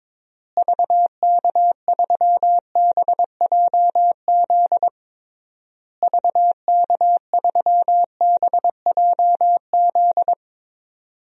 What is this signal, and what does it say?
Morse code, heard as a keyed tone, about 22 wpm, 700 Hz. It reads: VK3BJZ VK3BJZ
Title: DXCC-100 Call Signs - 6 Characters - Repeated 3x - 22wpm